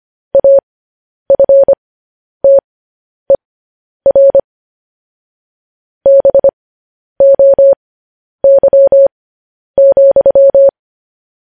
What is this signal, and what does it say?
Morse code, heard as a keyed tone, about 25 wpm, spaced as for 10 wpm, 550 Hz. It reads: AFTER BOY,
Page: news headlines in morse code at 10 wpm